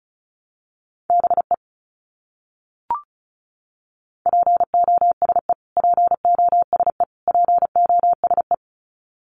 Morse code, heard as a keyed tone, about 34 wpm, 700 Hz. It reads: BE POSE POSE POSE
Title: Top 500 Words Repeated 35wpm